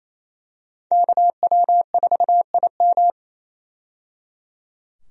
Morse code, heard as a keyed tone, about 28 wpm, 700 Hz. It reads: KW4IM
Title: USA Call Signs - 5 Characters - 28wpm